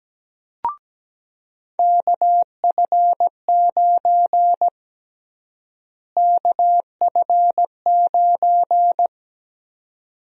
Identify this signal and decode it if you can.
Morse code, heard as a keyed tone, about 17 wpm, 700 Hz. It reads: KF9 KF9
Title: USA Call Sign Prefixes - Repeated 2x - 17wpm